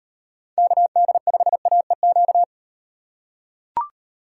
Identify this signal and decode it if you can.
Morse code, heard as a keyed tone, about 36 wpm, 700 Hz. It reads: KD5AEQ